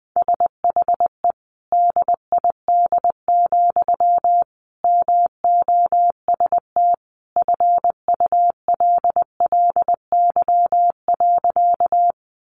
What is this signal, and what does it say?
Morse code, heard as a keyed tone, about 20 wpm, 700 Hz. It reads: SHE DID, MOST FULLY.
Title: Sentences from Top 1300 Words 20wpm